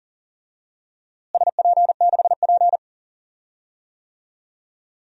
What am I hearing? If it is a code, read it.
Morse code, heard as a keyed tone, about 40 wpm, 700 Hz. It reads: SP6P
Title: DXCC-100 Call Signs - 4 Characters - Repeated 2x - 40wpm